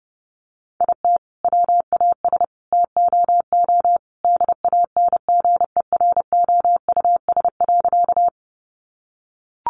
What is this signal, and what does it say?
Morse code, heard as a keyed tone, about 30 wpm, 700 Hz. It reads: IT WAS TOO DANGEROUS.
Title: Sentences from Top 1200 Words 30wpm